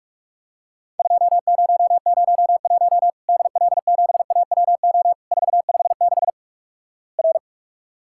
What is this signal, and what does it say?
Morse code, heard as a keyed tone, about 47 wpm, 700 Hz. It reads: 2001 DL7AWO 456 R
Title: Worked All Europe DX Contest - QTC Exchange - 45wpm